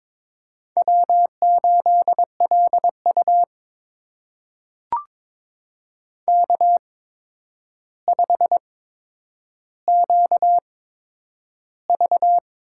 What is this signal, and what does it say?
Morse code, heard as a keyed tone, about 22 wpm, spaced as for 9 wpm, 700 Hz. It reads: W8LU K5QV
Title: USA Call Signs - 4 Characters - 8x Character Spacing - 22wpm